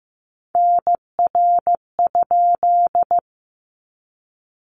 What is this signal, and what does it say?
Morse code, heard as a keyed tone, about 15 wpm, 700 Hz. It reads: NR?
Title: Common QSO Phrases 15wpm